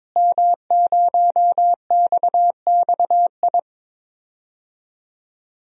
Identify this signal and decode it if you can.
Morse code, heard as a keyed tone, about 22 wpm, 700 Hz. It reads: M0XXI
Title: DXCC-100 Call Signs - 5 Characters - Repeated 3x - 22wpm